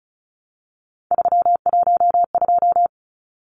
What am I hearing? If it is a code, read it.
Morse code, heard as a keyed tone, about 34 wpm, 700 Hz. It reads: 312
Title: Numbers - 3-Digit 35wpm